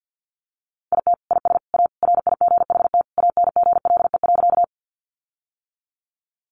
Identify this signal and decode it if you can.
Morse code, heard as a keyed tone, about 50 wpm, 700 Hz. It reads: IT IS A RIGHT ANGLE.
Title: Sentences from Top 1500 Words 50wpm